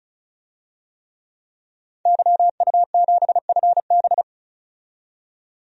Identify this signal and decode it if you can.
Morse code, heard as a keyed tone, about 34 wpm, 700 Hz. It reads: YU7FB
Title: DXCC-100 Call Signs - 5 Characters - 35wpm